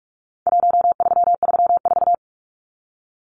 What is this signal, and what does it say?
Morse code, heard as a keyed tone, about 44 wpm, 700 Hz. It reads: 1334
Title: Numbers - Contest Style - 4-Digit 45wpm